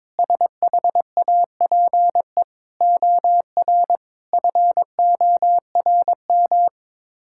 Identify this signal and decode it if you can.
Morse code, heard as a keyed tone, about 22 wpm, 700 Hz. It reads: SHAPE OR FORM
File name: practice / 22wpm / Sets of 6 Words - 22wpm